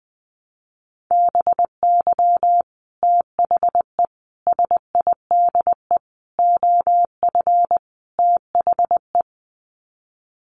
Morse code, heard as a keyed tone, about 20 wpm, 700 Hz. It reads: BY THE SIDE OF THE
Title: Sets of 5 Words Encore2 20wpm